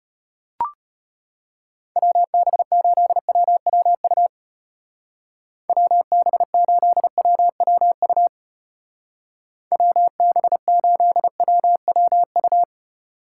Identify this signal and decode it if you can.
Morse code, heard as a keyed tone, about 32 wpm, 700 Hz. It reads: WB8WWU WB8WWU WB8WWU